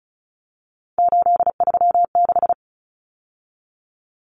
Morse code, heard as a keyed tone, about 34 wpm, 700 Hz. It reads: 836
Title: Numbers - 3-Digit 35wpm